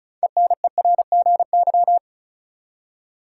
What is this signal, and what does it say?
Morse code, heard as a keyed tone, about 34 wpm, 700 Hz. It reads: ENERGY
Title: Six Letter Words - 35wpm